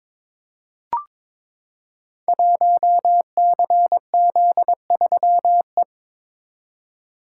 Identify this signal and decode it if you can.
Morse code, heard as a keyed tone, about 21 wpm, 700 Hz. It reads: 1CZ3E
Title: Code Groups - 5 Letter:Numbers - 22wpm